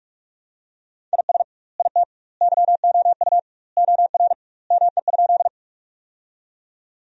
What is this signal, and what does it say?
Morse code, heard as a keyed tone, about 45 wpm, 700 Hz. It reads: IS IT YOU OR ME?
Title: 2x Word-Spacing - Sentences from Top 200 Words - 45wpm